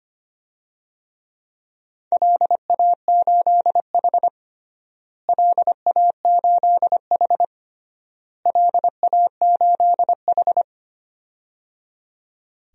Morse code, heard as a keyed tone, about 25 wpm, 700 Hz. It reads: LA8H LA8H LA8H
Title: DXCC-100 Call Signs - 4 Characters - Repeated 3x - 25wpm